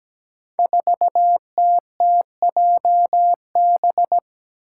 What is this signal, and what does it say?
Morse code, heard as a keyed tone, about 17 wpm, 700 Hz. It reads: 4TTJB
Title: Five Challenging Letter-Number Combinations 17wpm